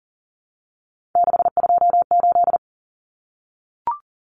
Morse code, heard as a keyed tone, about 40 wpm, 700 Hz. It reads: 628
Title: Common Cut Numbers - 3-Digit 40wpm